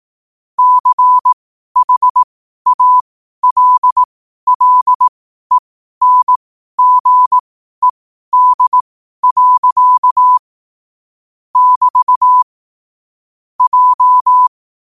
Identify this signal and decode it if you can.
Morse code, heard as a keyed tone, about 18 wpm, spaced as for 12 wpm, 1000 Hz. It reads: CHALLENGED. <BT> J
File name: qotdfiles / 12wpm